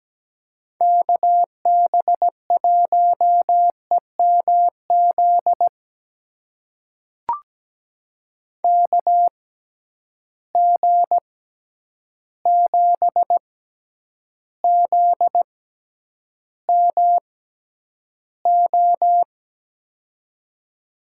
Morse code, heard as a keyed tone, about 17 wpm, spaced as for 9 wpm, 700 Hz. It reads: KB1EMZ KG7ZMO
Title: USA Call Signs - 6 Characters - 6x Character Spacing - 17wpm